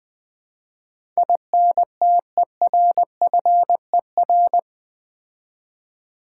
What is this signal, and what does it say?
Morse code, heard as a keyed tone, about 20 wpm, 700 Hz. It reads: INTERFER
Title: Build A Word - 4 Letters At A Time - 12-Letter Words - 20wpm